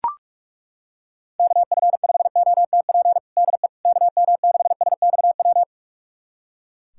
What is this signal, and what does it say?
Morse code, heard as a keyed tone, about 41 wpm, 700 Hz. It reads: KF5OTP DE KM6SXW
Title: USA Callsign DE Callsign 45wpm